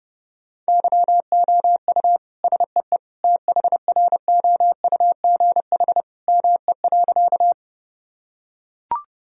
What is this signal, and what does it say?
Morse code, heard as a keyed tone, about 30 wpm, 700 Hz. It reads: YOU SEE THROUGH ME.
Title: Sentences from Top 1000 Words - Review Part 1 - 30wpm